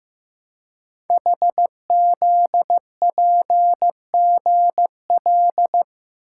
Morse code, heard as a keyed tone, about 15 wpm, 700 Hz. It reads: HZPGL
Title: Code Groups - 5 Letters - 15wpm